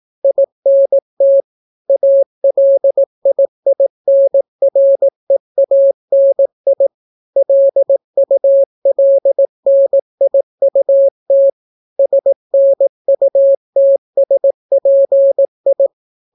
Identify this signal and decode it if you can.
Morse code, heard as a keyed tone, about 18 wpm, 550 Hz. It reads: INT ALIINREANI LULNIUT SNUTSPI